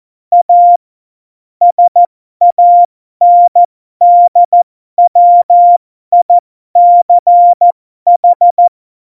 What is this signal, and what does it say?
Morse code, heard as a keyed tone, about 14 wpm, spaced as for 12 wpm, 700 Hz. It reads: A SANDWICH
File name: Morsekurs / Witze